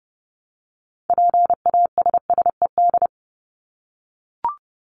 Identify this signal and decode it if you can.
Morse code, heard as a keyed tone, about 30 wpm, 700 Hz. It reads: PASSED
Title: ED Suffix - Top 100 Words with Base Words 30wpm